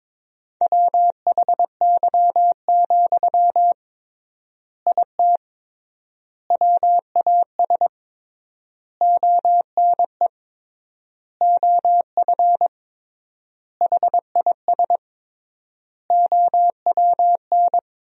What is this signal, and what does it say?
Morse code, heard as a keyed tone, about 22 wpm, 700 Hz. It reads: WHY, IT WAS ONE OF HIS OWN
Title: 3x Word-Spacing - Sentences from Top 300 Words - 22wpm